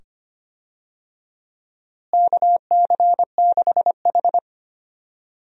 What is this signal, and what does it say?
Morse code, heard as a keyed tone, about 25 wpm, 700 Hz. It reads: KC6H
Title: USA Call Signs - 4 Characters - Repeated 2x - 25wpm